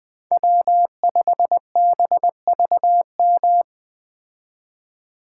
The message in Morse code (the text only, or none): W5BVM